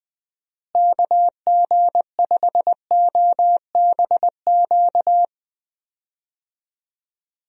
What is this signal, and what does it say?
Morse code, heard as a keyed tone, about 20 wpm, 700 Hz. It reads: KG5OBQ
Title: USA Call Signs - 6 Characters - Repeated 2x - 20wpm